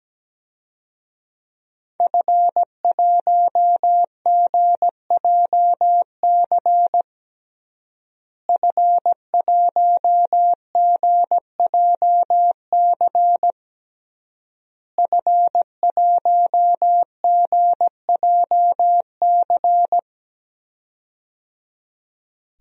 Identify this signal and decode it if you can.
Morse code, heard as a keyed tone, about 17 wpm, 700 Hz. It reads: F1GJC F1GJC F1GJC
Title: DXCC-100 Call Signs - 5 Characters - Repeated 3x - 17wpm